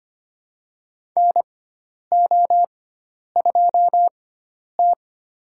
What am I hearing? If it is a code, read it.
Morse code, heard as a keyed tone, about 25 wpm, spaced as for 10 wpm, 700 Hz. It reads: NO2T
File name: Lesson 16g - Letter c - Call Signs 25wpm